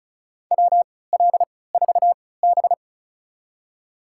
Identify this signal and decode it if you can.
Morse code, heard as a keyed tone, about 36 wpm, spaced as for 20 wpm, 700 Hz. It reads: WL4B
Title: USA Call Signs - 3x Character Spacing - 35wpm